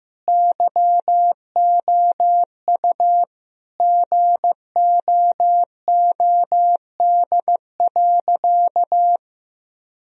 Morse code, heard as a keyed tone, about 15 wpm, 700 Hz. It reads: YOU GOOD.
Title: Sentences from Top 1000 Words - Review Part 1 - 15wpm